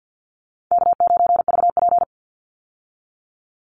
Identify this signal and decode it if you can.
Morse code, heard as a keyed tone, about 50 wpm, 700 Hz. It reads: K9VP
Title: Call Signs Repeated 2x - 50wpm